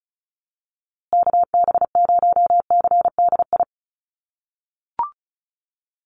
Morse code, heard as a keyed tone, about 34 wpm, 700 Hz. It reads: KB0CDI
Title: USA Call Signs Repeated 2x 35wpm